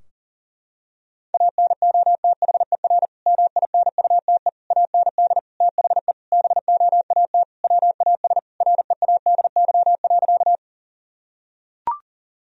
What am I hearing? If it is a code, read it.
Morse code, heard as a keyed tone, about 40 wpm, 700 Hz. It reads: ANOTHER MINUTE AND THE BOAT WAS READY.
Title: Sentences from Top 1000 Words - Review Part 3 - 40wpm